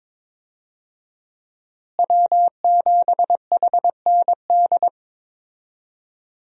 Morse code, heard as a keyed tone, about 22 wpm, 700 Hz. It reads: W7HND
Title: USA Call Signs - 5 Characters - Repeated 2x - 22wpm